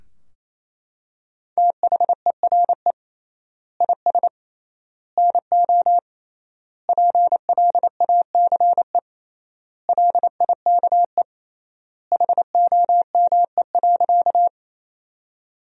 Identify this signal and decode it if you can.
Morse code, heard as a keyed tone, about 28 wpm, 700 Hz. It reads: THERE IS NO PLACE LIKE HOME.
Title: 3x Word-Spacing - Sentences from Top 200 Words - 28wpm